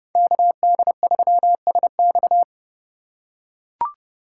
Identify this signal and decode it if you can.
Morse code, heard as a keyed tone, about 32 wpm, 700 Hz. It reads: KD3SX
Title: USA Call Signs - 5 Characters - 6x Character Spacing - 30wpm